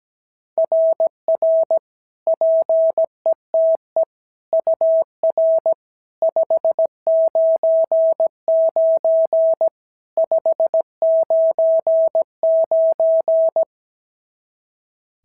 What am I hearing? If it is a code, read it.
Morse code, heard as a keyed tone, about 15 wpm, 650 Hz. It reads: RR PETE UR 599 599
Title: SOTA QSO Practice 17wpm